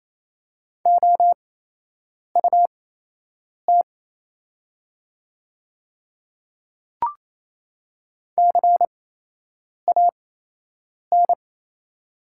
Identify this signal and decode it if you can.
Morse code, heard as a keyed tone, about 28 wpm, spaced as for 8 wpm, 700 Hz. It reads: OUT CAN
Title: 8x Character Spacing - Top 200 Words - 28wpm